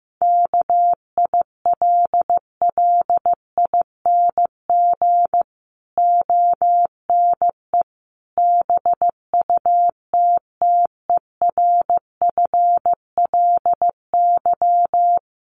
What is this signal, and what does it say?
Morse code, heard as a keyed tone, about 15 wpm, 700 Hz. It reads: KILLING ONE BUTTERFLY
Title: A Sound Of Thunder 15wpm